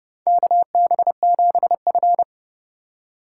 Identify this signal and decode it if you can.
Morse code, heard as a keyed tone, about 30 wpm, 700 Hz. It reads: KB7F
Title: USA Call Signs - 3x Character Spacing - 30wpm